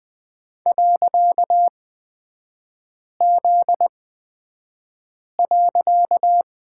Morse code, heard as a keyed tone, about 20 wpm, spaced as for 5 wpm, 700 Hz. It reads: .Z.
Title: 255 Lesson 41e - Period - Three Characters 20wpm